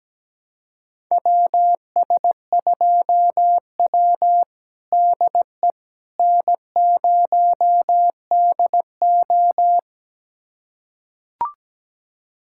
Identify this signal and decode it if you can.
Morse code, heard as a keyed tone, about 17 wpm, 700 Hz. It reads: WS2W DE N0DO
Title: USA Callsign DE Callsign - 4 Characters - 17wpm